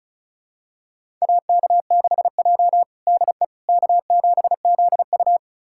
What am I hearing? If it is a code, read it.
Morse code, heard as a keyed tone, about 36 wpm, 700 Hz. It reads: AK6J DE K7ZU
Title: USA Callsign DE Callsign - 4 Characters - 35wpm